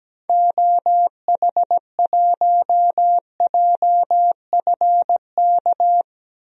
Morse code, heard as a keyed tone, about 17 wpm, 700 Hz. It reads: OH1JFK
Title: DXCC-100 Call Signs - 6 Characters - 17wpm